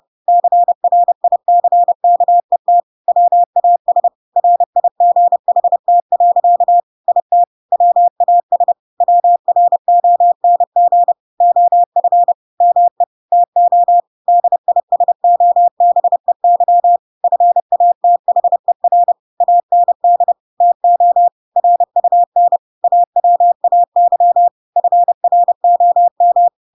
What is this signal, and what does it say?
Morse code, heard as a keyed tone, about 30 wpm, 700 Hz. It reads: CRICKET WAS RIGHT. IT WAS WRONG OF ME TO DISOBEY FATHER AND TO RUN AWAY FROM
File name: Pinocchio / Pinocchio ch5 30 30 ik1hge